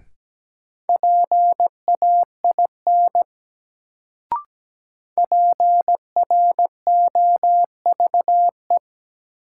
Morse code, heard as a keyed tone, about 17 wpm, 700 Hz. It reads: PAIN PROVE
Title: Sentences from Top 1600 Words - New Word Drill - 17wpm